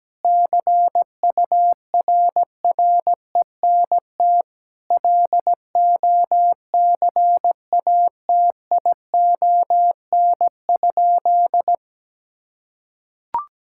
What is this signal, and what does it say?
Morse code, heard as a keyed tone, about 17 wpm, 700 Hz. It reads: CURRENT LOCATION?
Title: Sentences from Top 1000 Words - Review Part 6 - 17wpm